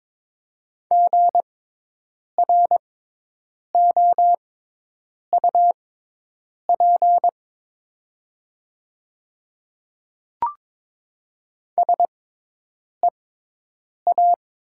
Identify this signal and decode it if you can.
Morse code, heard as a keyed tone, about 22 wpm, spaced as for 8 wpm, 700 Hz. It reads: GROUP SEA
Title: 6x Character Spacing - Top 500 Words - 22wpm